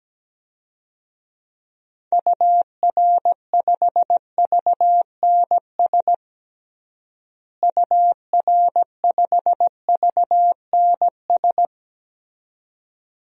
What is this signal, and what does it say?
Morse code, heard as a keyed tone, about 17 wpm, 700 Hz. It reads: UR5VNS UR5VNS
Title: DXCC-100 Call Signs - 6 Characters - Repeated 2x - 17wpm